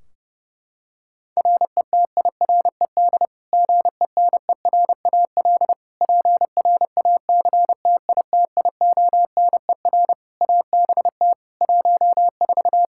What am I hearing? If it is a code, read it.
Morse code, heard as a keyed tone, about 32 wpm, 700 Hz. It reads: RETIRED GENERAL PRACTITIONER ABT 14
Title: Practice QSO - Exchange 3 - More Info - 30wpm